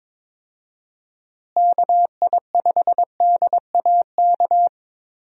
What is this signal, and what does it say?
Morse code, heard as a keyed tone, about 22 wpm, 700 Hz. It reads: KI5DAK
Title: USA Call Signs - 6 Characters - Repeated 3x - 22wpm